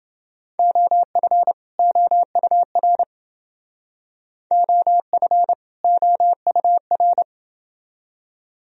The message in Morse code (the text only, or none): OF OUR OF OUR